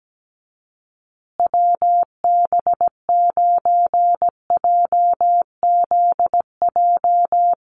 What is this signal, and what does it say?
Morse code, heard as a keyed tone, about 17 wpm, 700 Hz. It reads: WB9JZJ
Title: USA Call Signs Repeated 3x 17wpm